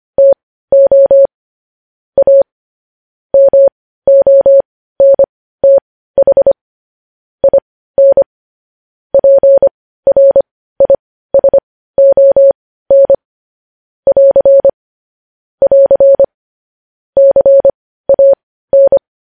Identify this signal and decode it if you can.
Morse code, heard as a keyed tone, about 25 wpm, spaced as for 15 wpm, 550 Hz. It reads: TO A MONTH IN PRISON <AR> <AR> CAN